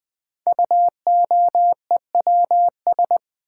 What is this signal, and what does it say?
Morse code, heard as a keyed tone, about 20 wpm, 700 Hz. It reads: UOEWS
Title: Code Groups - 5 Letters - 20wpm